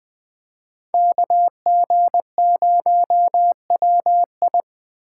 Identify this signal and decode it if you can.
Morse code, heard as a keyed tone, about 21 wpm, 700 Hz. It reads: KG0WI
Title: Speed Racing - USA Call Signs 17wpm